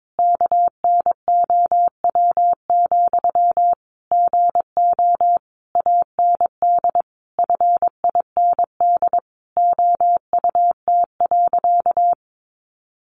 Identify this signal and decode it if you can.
Morse code, heard as a keyed tone, about 22 wpm, 700 Hz. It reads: KNOW, GO AND FIND OUT.
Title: Sentences from Top 200 Words 22wpm